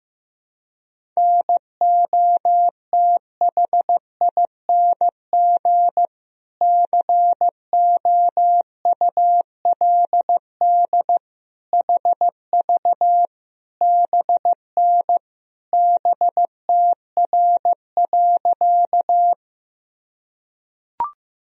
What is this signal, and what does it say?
Morse code, heard as a keyed tone, about 15 wpm, 700 Hz. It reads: NOTHING COULD HV BN BTR.
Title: Sentences from Top 500 Words and Abbreviations 15wpm